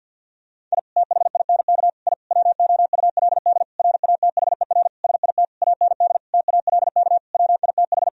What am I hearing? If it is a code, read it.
Morse code, heard as a keyed tone, about 51 wpm, 700 Hz. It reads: I THINK I WOULD RATHER SIT AND TALK WITH
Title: Sentences from Top 1000 Words - Review Part 4 - 50wpm